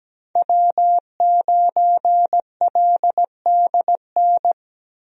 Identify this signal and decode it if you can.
Morse code, heard as a keyed tone, about 17 wpm, 700 Hz. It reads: W9LDN
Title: USA Call Signs - 5 Characters - Repeated 3x - 17wpm